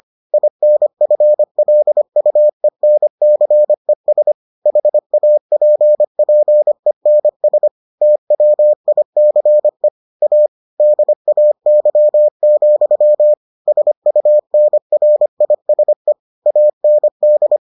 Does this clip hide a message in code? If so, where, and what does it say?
Yes: Morse code from 0.3 to 17.6 s, heard as a keyed tone, about 25 wpm, 600 Hz. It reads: INFLUENCES HAPPENS TWICE A DAY, SUNRISE AND